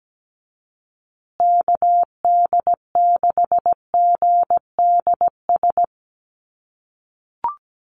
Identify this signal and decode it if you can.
Morse code, heard as a keyed tone, about 17 wpm, 700 Hz. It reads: KD6GDS
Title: USA Call Signs Repeated 3x 17wpm